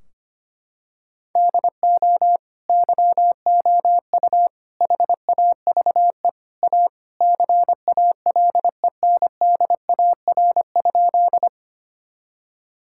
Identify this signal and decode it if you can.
Morse code, heard as a keyed tone, about 25 wpm, 700 Hz. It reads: DO YOU HAVE A CALENDAR?
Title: Sentences from Top 1600 Words 25wpm